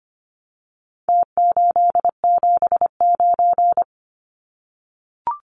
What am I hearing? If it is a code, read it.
Morse code, heard as a keyed tone, about 25 wpm, 700 Hz. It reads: T879
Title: Numbers - Contest Style - 4-Digit 25wpm